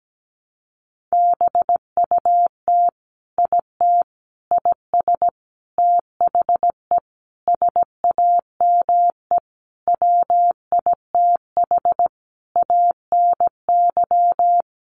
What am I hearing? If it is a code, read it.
Morse code, heard as a keyed tone, about 17 wpm, 700 Hz. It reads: BUT IT IS THE SAME WITH ANY
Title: Sentences from Top 300 Words 17wpm